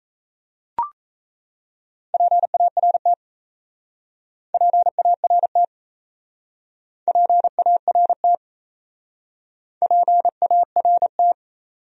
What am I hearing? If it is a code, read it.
Morse code, heard as a keyed tone, about 37 wpm, 700 Hz. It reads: PART PART PART PART